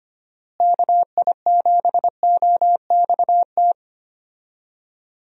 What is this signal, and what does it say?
Morse code, heard as a keyed tone, about 25 wpm, 700 Hz. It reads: KI7OXT
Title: USA Call Signs - 6 Characters - Repeated 2x - 25wpm